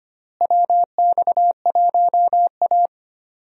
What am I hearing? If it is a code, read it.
Morse code, heard as a keyed tone, about 25 wpm, 700 Hz. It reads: WX1A
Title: USA Call Signs - 4 Characters - Repeated 2x - 25wpm